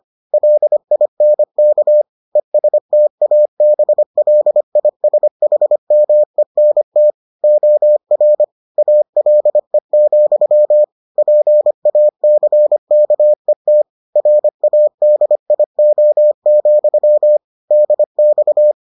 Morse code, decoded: LINK ESTABLISHMENT OR ALE, PACKET RADIO, DX

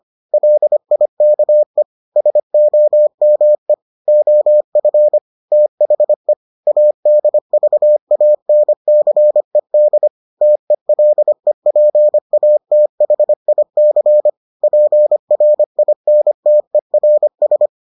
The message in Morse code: LIKE SOME OF THE ADVANCED TELEPATHIC PRINTERS

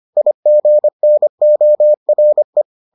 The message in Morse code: IGNORE

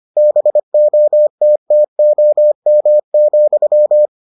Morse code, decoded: BOTTOM,

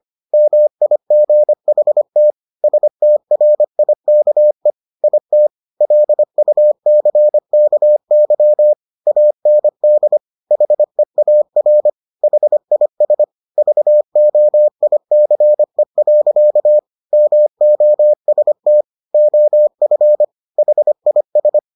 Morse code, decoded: MIGHT STRIKE IT LUCKY AND HEAR HIS VOICE. MOST OF HIS

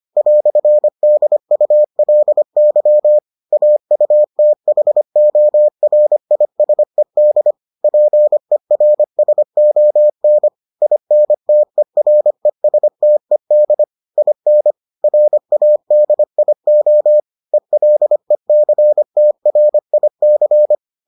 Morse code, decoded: "DULY AUTHORISED PERSON INTERESTED IN RADIO ELECTRIC

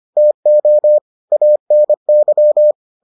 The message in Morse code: TO ANY

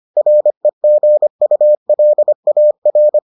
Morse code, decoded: REGULAR

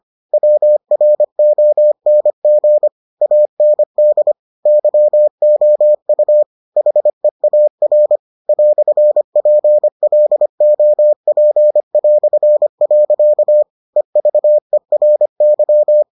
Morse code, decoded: WRONG AND YOU HEAR "PLOP". EVERY

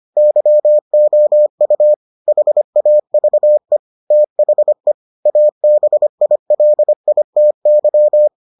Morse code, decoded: YOU HAVE THE ABILITY

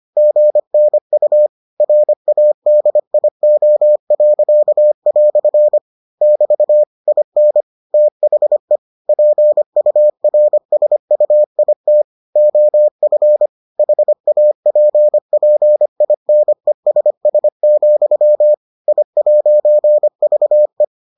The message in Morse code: GNU RADIO." <BT> IN THE PURSUIT OF HAPPINESS, I'VE

